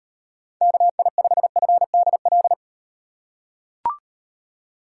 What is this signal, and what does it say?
Morse code, heard as a keyed tone, about 38 wpm, 700 Hz. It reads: KI5FDL